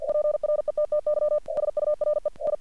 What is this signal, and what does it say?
Morse code, heard as a keyed tone, about 48 wpm, 600 Hz. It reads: PRETTY SURE I
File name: archived / CRSnet